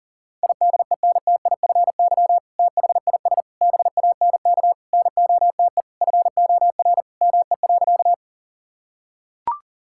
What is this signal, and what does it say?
Morse code, heard as a keyed tone, about 40 wpm, 700 Hz. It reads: IDENTIFY THIS BANK NOTE FOR ME.
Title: Sentences from Top 1000 Words - Review Part 6 - 40wpm